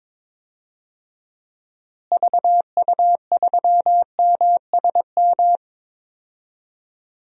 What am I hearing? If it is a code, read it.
Morse code, heard as a keyed tone, about 22 wpm, 700 Hz. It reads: VU3MSM
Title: DXCC-100 Call Signs - 6 Characters - Repeated 3x - 22wpm